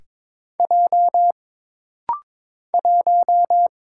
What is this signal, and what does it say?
Morse code, heard as a keyed tone, about 22 wpm, 700 Hz. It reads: J 1
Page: This rapid-fire timing is geared to push your ICR (Instant Character Recognition) skills.